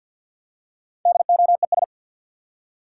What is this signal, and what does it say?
Morse code, heard as a keyed tone, about 53 wpm, 700 Hz. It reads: DOES
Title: Four Letter Words - 50wpm